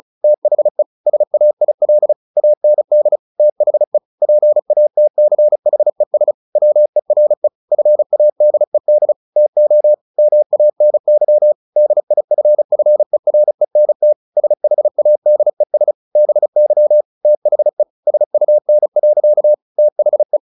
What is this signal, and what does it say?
Morse code, heard as a keyed tone, about 34 wpm, 600 Hz. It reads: THE SAIL AND THE PATCHES WERE FADED TO MANY DIFFERENT SHADES BY THE SUN. THE
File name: the old man and the sea / the old man and the sea 1 35wpm